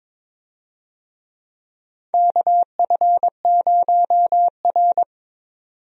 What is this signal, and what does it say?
Morse code, heard as a keyed tone, about 22 wpm, 700 Hz. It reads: KF0R
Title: USA Call Signs - 4 Characters - 22wpm